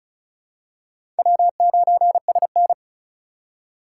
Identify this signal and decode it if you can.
Morse code, heard as a keyed tone, about 36 wpm, 700 Hz. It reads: W9SN
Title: USA Call Signs - 4x Character Spacing - 35wpm